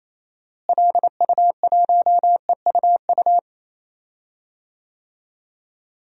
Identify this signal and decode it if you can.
Morse code, heard as a keyed tone, about 28 wpm, 700 Hz. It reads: LU1EUU
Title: DXCC-100 Call Signs - 6 Characters - Repeated 2x - 28wpm